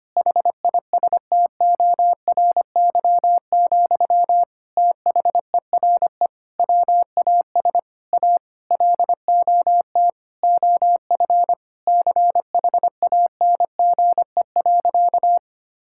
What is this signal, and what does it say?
Morse code, heard as a keyed tone, about 25 wpm, 700 Hz. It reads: HISTORY, THERE WAS A LOT OF CHANGE.
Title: Sentences from Top 1000 Words - Review Part 5 - 25wpm